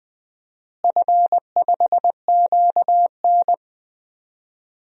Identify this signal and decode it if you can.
Morse code, heard as a keyed tone, about 19 wpm, 700 Hz. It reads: F5QN
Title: DXCC-100 Call Signs - 4 Characters - Repeated 3x - 20wpm